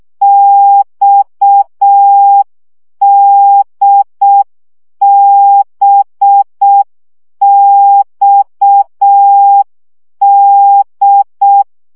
Morse code, heard as a keyed tone, about 6 wpm, 800 Hz. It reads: XDBXD
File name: CW gyakorlas d b x 5 baud